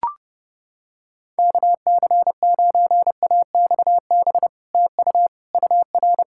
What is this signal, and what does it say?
Morse code, heard as a keyed tone, about 29 wpm, 700 Hz. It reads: KC9AXB TU UR